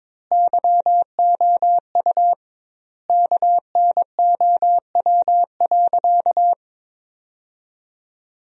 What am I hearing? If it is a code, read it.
Morse code, heard as a keyed tone, about 22 wpm, 700 Hz. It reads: YOU KNOW.
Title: 2x Word-Spacing - Sentences from Top 100 Words - 22wpm